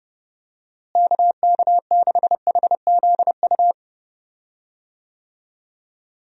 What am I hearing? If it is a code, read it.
Morse code, heard as a keyed tone, about 30 wpm, 700 Hz. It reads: KK6HZU
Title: USA Call Signs - 6 Characters - Repeated 3x - 30wpm